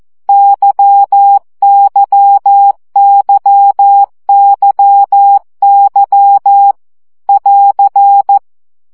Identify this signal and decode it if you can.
Morse code, heard as a keyed tone, about 14 wpm, 800 Hz. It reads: YYYYY <AR>
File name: CW gyakorlas y 12 baud